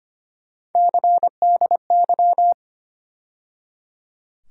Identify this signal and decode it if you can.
Morse code, heard as a keyed tone, about 25 wpm, 700 Hz. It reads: CDY
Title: USA Call Sign Postfixes - 3 Letters 25wpm